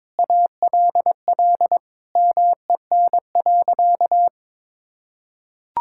Morse code, heard as a keyed tone, about 22 wpm, 700 Hz. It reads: ALL MEN.
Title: Sentences from Top 1000 Words - Review Part 6 - 22wpm